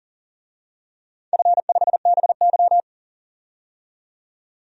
Morse code, heard as a keyed tone, about 40 wpm, 700 Hz. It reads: F5BY
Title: DXCC-100 Call Signs - 4 Characters - 40wpm